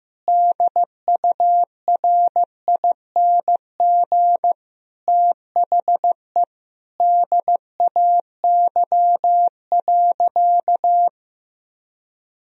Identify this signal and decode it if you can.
Morse code, heard as a keyed tone, about 15 wpm, 700 Hz. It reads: DURING THE DAY.
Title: Sentences from Top 1000 Words - Review Part 2 - 15wpm